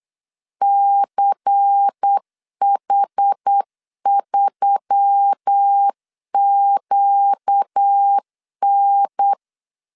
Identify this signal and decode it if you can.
Morse code, heard as a keyed tone, about 8 wpm, 800 Hz. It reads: CH3QN
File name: CW / DET / 1 9 1500C S A 8wpm 5637